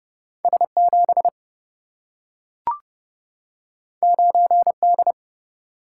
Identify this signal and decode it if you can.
Morse code, heard as a keyed tone, about 27 wpm, 700 Hz. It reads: S7 9D